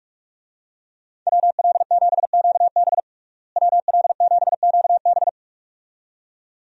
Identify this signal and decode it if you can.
Morse code, heard as a keyed tone, about 43 wpm, 700 Hz. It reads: WL7QB WL7QB
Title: USA Call Signs - 5 Characters - Repeated 2x - 45wpm